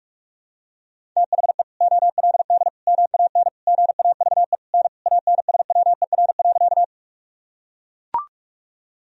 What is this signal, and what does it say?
Morse code, heard as a keyed tone, about 41 wpm, 700 Hz. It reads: THE OLD MAN GAVE N ANSWER.
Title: Sentences from Top 500 Words and Abbreviations 45wpm